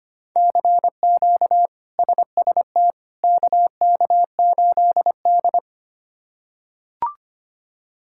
Morse code, heard as a keyed tone, about 25 wpm, 700 Hz. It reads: CQ SST KK8D
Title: K1USN SST QSO Practice - 25wpm